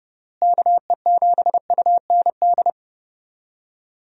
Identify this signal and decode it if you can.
Morse code, heard as a keyed tone, about 30 wpm, 700 Hz. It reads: KE7UND
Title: USA Call Signs - 6 Characters - Repeated 3x - 30wpm